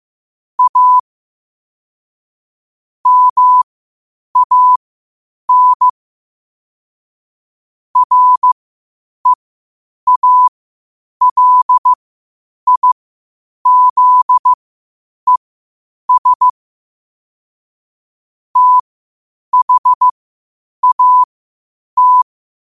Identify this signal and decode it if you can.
Morse code, heard as a keyed tone, about 15 wpm, spaced as for 8 wpm, 1000 Hz. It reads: A MAN REALIZES THAT
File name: qotdfiles / 8wpm